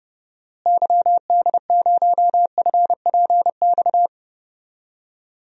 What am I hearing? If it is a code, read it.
Morse code, heard as a keyed tone, about 30 wpm, 700 Hz. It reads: YD0FPX
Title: DXCC-100 Call Signs - 6 Characters - Repeated 3x - 30wpm